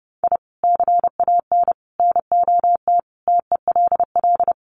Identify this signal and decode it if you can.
Morse code, heard as a keyed tone, about 28 wpm, 700 Hz. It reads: I CAN NOT TELL
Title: Sets of 5 Words 30wpm